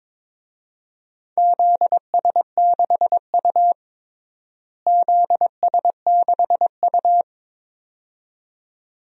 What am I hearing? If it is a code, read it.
Morse code, heard as a keyed tone, about 22 wpm, 700 Hz. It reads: ZS6U ZS6U
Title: DXCC-100 Call Signs - 4 Characters - Repeated 2x - 22wpm